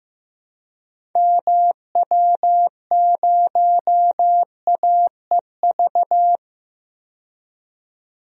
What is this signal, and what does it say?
Morse code, heard as a keyed tone, about 15 wpm, 700 Hz. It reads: MW0AEV
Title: DXCC-100 Call Signs - 6 Characters - 15wpm